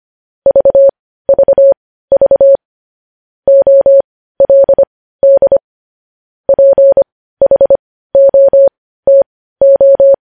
Morse code, heard as a keyed tone, about 25 wpm, spaced as for 15 wpm, 550 Hz. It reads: VVV OLD PHOTO
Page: news-headlines-in-morse-code-at-15-wpm